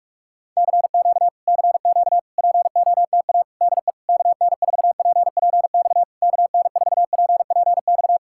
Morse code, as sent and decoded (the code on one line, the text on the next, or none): -.-. --.- / -.-. --.- / .--. --- - .- / -.. / -.- -. ....- .--. .--. -..- / -.- -. ....- .--. .--. -..-
CQ CQ POTA DE KN4PPX KN4PPX